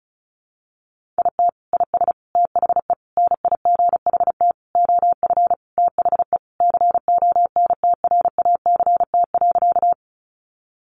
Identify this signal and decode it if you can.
Morse code, heard as a keyed tone, about 34 wpm, 700 Hz. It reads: IT IS THE NIGHT OF THE CONTRACT.
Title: Sentences from Top 1400 Words 35wpm